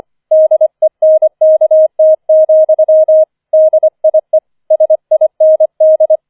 Morse code, heard as a keyed tone, about 25 wpm, 630 Hz. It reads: DENKT, DIE SIND
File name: 25wpm